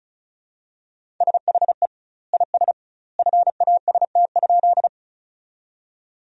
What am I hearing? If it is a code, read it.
Morse code, heard as a keyed tone, about 36 wpm, 700 Hz. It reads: SHE IS FAST?
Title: 2x Word-Spacing - Sentences from Top 400 Words - 35wpm